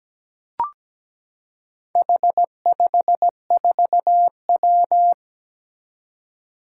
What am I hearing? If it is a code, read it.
Morse code, heard as a keyed tone, about 17 wpm, 700 Hz. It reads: H54W
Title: Four Challenging Letter-Number Combinations 17wpm